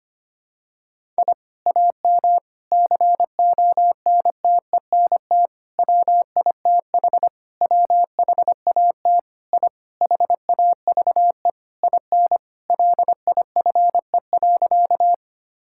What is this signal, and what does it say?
Morse code, heard as a keyed tone, about 25 wpm, 700 Hz. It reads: I AM CONTENT WITH WHAT I HAVE IN LIFE.
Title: Sentences from Top 1000 Words - Review Part 6 - 25wpm